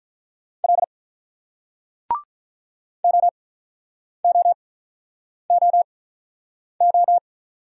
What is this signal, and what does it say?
Morse code, heard as a keyed tone, about 50 wpm, spaced as for 9 wpm, 700 Hz. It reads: L OOOO